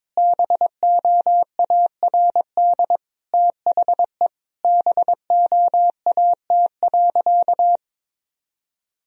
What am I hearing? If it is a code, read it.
Morse code, heard as a keyed tone, about 22 wpm, 700 Hz. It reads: BOARD THE BOAT.
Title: Sentences from Top 1000 Words - Review Part 4 - 22wpm